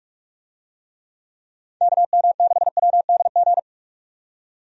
Morse code, heard as a keyed tone, about 41 wpm, 700 Hz. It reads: KM6WDG
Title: USA Call Signs - 6 Characters - 45wpm